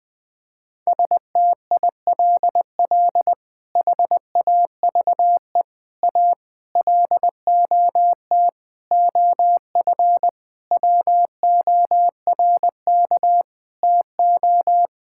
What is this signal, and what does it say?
Morse code, heard as a keyed tone, about 20 wpm, 700 Hz. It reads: STILL HAVE A LOT OF WORK TO
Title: Sets of 8 Words - 20wpm